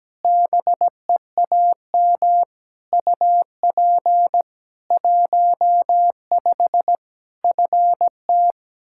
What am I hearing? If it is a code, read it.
Morse code, heard as a keyed tone, about 17 wpm, 700 Hz. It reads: BEAM UP 15 FT
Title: Practice QSO - Exchange 2 - Info - 17wpm